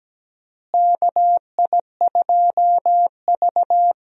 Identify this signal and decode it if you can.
Morse code, heard as a keyed tone, about 17 wpm, 700 Hz. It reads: KI2V
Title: USA Call Signs - 4 Characters - Repeated 3x - 17wpm